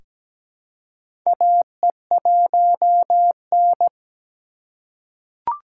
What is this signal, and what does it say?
Morse code, heard as a keyed tone, about 17 wpm, 700 Hz. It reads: AE1N
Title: K1USN SST QSO Practice - 17wpm